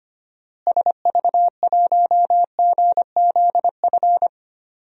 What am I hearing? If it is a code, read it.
Morse code, heard as a keyed tone, about 25 wpm, 700 Hz. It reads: SV1GZF
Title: DXCC-100 Call Signs - 6 Characters - Repeated 3x - 25wpm